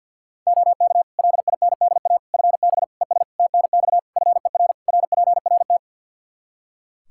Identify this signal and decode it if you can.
Morse code, heard as a keyed tone, about 49 wpm, 700 Hz. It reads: OK LINDA FB ES TNX FER RPRT
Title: Practice QSO - Exchange 2 - Info - 50wpm